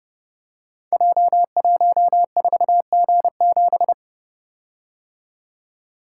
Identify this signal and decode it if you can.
Morse code, heard as a keyed tone, about 30 wpm, 700 Hz. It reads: J14G7
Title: Code Groups - 5 Letter:Numbers - 30wpm